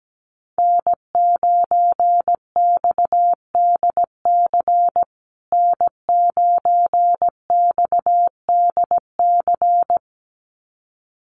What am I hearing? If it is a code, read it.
Morse code, heard as a keyed tone, about 17 wpm, 700 Hz. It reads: N9XDC N9XDC
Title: USA Call Signs Repeated 2x 17wpm